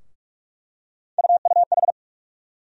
Morse code, heard as a keyed tone, about 46 wpm, 700 Hz. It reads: UUH
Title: USA Call Sign Postfixes - 3 Letters 45wpm